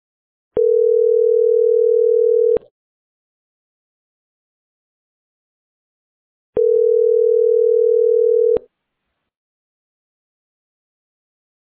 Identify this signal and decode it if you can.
Ringback tone (North America), 440 + 480 Hz.